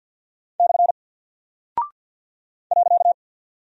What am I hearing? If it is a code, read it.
Morse code, heard as a keyed tone, about 49 wpm, 700 Hz. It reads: / .
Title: Single Character (Letter:Number:Punctuation:Prosign) Alt 50wpm